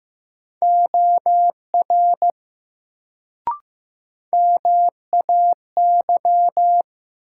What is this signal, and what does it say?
Morse code, heard as a keyed tone, about 15 wpm, 700 Hz. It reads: OR MAY
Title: Sentences from Top 100 Words - New Word Drill - 15wpm